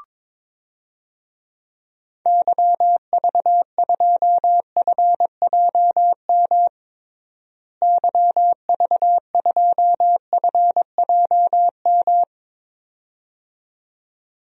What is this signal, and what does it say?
Morse code, heard as a keyed tone, about 22 wpm, 700 Hz. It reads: YV2FJM YV2FJM
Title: DXCC-100 Call Signs - 6 Characters - Repeated 2x - 22wpm